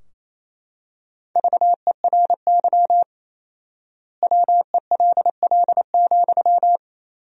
Morse code, heard as a keyed tone, about 28 wpm, 700 Hz. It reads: VERY WELL,
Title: 4x Word-Spacing - Sentences from Top 200 Words - 28wpm